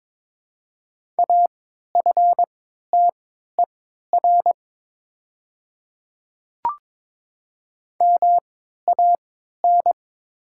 Morse code, heard as a keyed tone, about 22 wpm, spaced as for 12 wpm, 700 Hz. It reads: AFTER MAN
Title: 3x Character Spacing - Top 300 Words - 22wpm